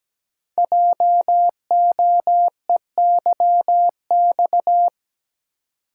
Morse code, heard as a keyed tone, about 17 wpm, 700 Hz. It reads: JOEYX
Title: Code Groups - 5 Letter:Numbers - 17wpm